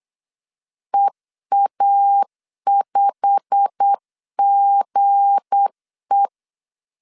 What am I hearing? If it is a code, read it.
Morse code, heard as a keyed tone, about 8 wpm, 800 Hz. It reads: EA5GE
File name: LEZIONI / 8 YJ45 / 8 YJ45 8wpm 3214